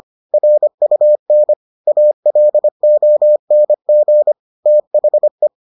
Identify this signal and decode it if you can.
Morse code, heard as a keyed tone, about 25 wpm, 600 Hz. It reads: RUN ALONG THE